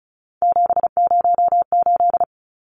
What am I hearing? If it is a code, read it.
Morse code, heard as a keyed tone, about 34 wpm, 700 Hz. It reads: 708
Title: Numbers - 3-Digit 35wpm